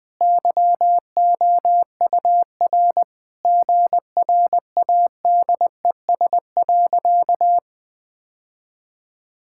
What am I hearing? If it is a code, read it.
Morse code, heard as a keyed tone, about 19 wpm, 700 Hz. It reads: YOUR GRADES.
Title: Sentences with 3-Word Phrases - 20wpm